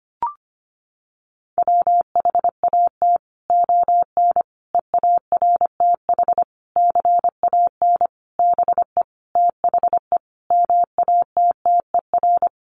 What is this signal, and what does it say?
Morse code, heard as a keyed tone, about 25 wpm, 700 Hz. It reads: WHAT ON EARTH CAN BE THE MATTER
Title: Sentences from Top 700 Words 25wpm